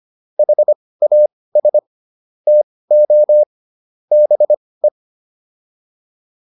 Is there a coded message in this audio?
Yes: Morse code from 0.4 to 4.9 s, heard as a keyed tone, about 25 wpm, spaced as for 18 wpm, 600 Hz. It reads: HAS TO BE